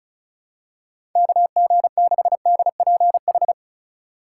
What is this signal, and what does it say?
Morse code, heard as a keyed tone, about 36 wpm, 700 Hz. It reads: KG6DPH
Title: USA Call Signs - 3x Character Spacing - 35wpm